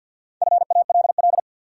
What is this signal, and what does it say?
Morse code, heard as a keyed tone, about 49 wpm, 700 Hz. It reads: FALL
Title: Sentences from Top 500 Words - New Word Drill - 50wpm